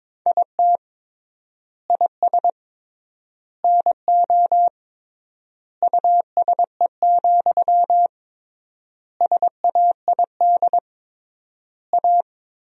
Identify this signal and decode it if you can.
Morse code, heard as a keyed tone, about 22 wpm, 700 Hz. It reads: IT IS NO USE, SAID A
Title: 3x Word-Spacing - Sentences from Top 500 Words - 22wpm